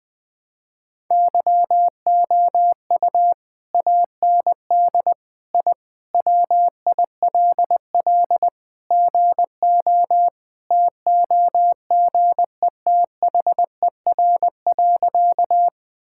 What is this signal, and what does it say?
Morse code, heard as a keyed tone, about 21 wpm, 700 Hz. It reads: YOU AND I WILL GO TOGETHER.
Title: Sentences with 2-Word Phrases - 20wpm